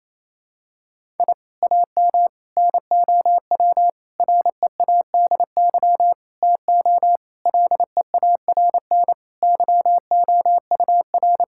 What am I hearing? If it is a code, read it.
Morse code, heard as a keyed tone, about 28 wpm, 700 Hz. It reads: I AM NOW READY TO LEARN YOUR
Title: Sentences from Top 1000 Words - Review Part 6 - 28wpm